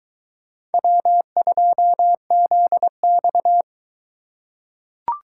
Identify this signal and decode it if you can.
Morse code, heard as a keyed tone, about 21 wpm, 700 Hz. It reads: W2ZX